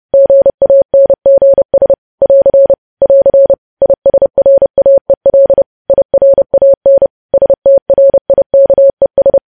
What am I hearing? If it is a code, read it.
Morse code, heard as a keyed tone, about 30 wpm, 550 Hz. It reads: GANGS <AR> <AR> ISRAEL IRAN STRIKES
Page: News Headlines in Morse Code at 30 WPM